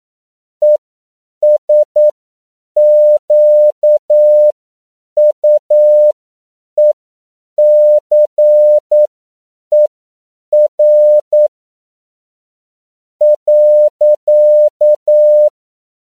Morse code, heard as a keyed tone, about 9 wpm, spaced as for 6 wpm, 600 Hz. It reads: ESQUECER.